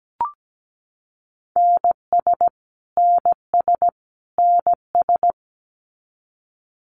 Morse code, heard as a keyed tone, about 17 wpm, 700 Hz. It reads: NS NS NS
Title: Canadian Provinces and Territories - Repeated 3x - 17wpm